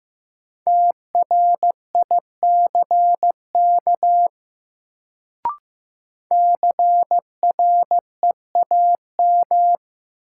Morse code, heard as a keyed tone, about 15 wpm, 700 Hz. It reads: TRICK CREAM
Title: Sentences from Top 1500 Words - New Word Drill - 15wpm